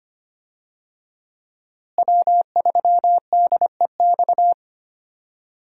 Morse code, heard as a keyed tone, about 25 wpm, 700 Hz. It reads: W3DEX
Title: USA Call Signs - 5 Characters - Repeated 2x - 25wpm